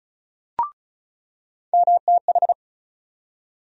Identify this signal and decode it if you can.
Morse code, heard as a keyed tone, about 36 wpm, 700 Hz. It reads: MTH
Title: USA Call Sign Postfixes - 3 Letters 35wpm